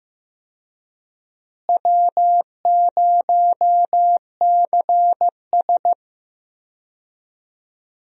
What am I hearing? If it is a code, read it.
Morse code, heard as a keyed tone, about 15 wpm, 700 Hz. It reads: W0CS
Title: USA Call Signs - 4 Characters - Repeated 2x - 15wpm